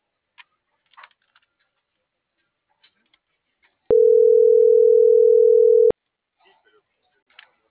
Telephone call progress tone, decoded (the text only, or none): ringback tone